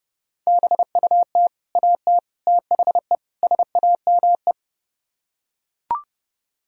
Morse code, heard as a keyed tone, about 30 wpm, 700 Hz. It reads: BUT AT THE SAME